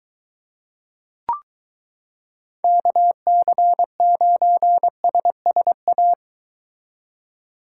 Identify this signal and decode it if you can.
Morse code, heard as a keyed tone, about 22 wpm, 700 Hz. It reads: KC9SSA